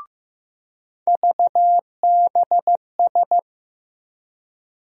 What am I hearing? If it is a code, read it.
Morse code, heard as a keyed tone, about 15 wpm, 700 Hz. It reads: VBS
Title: USA Call Sign Postfixes - 3 Letters 15wpm